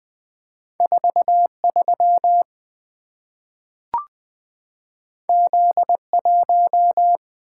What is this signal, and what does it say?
Morse code, heard as a keyed tone, about 20 wpm, 700 Hz. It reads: 43 Z1
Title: Two Letter-Number Combinations - 20wpm